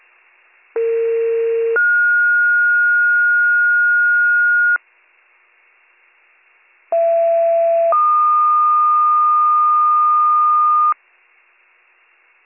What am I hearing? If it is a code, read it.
Two-tone page, heard as a one-second tone, then a three-second tone, about 460 then 1430 Hz, about 670 then 1220 Hz. It is heard from 0.8 to 10.9 s.